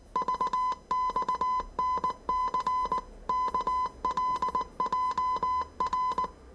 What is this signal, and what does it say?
Morse code, heard as a keyed tone, about 19 wpm, 1010 Hz. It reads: VXNC KLJR
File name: 业余无线电 / ●未分类 / 字码80